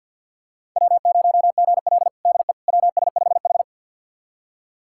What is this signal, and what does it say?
Morse code, heard as a keyed tone, about 50 wpm, 700 Hz. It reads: W0GL DE WS5H
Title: USA Callsign DE Callsign - 4 Characters - 50wpm